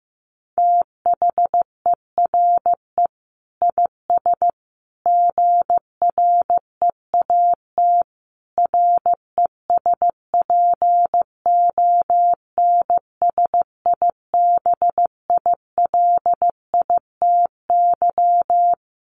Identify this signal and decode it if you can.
Morse code, heard as a keyed tone, about 15 wpm, 700 Hz. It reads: THERE IS GREAT RESPONSIBILITY
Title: Sentences from Top 1200 Words 15wpm